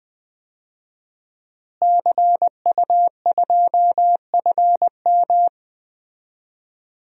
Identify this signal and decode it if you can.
Morse code, heard as a keyed tone, about 20 wpm, 700 Hz. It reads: CU2FM
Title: DXCC-100 Call Signs - 5 Characters - Repeated 3x - 20wpm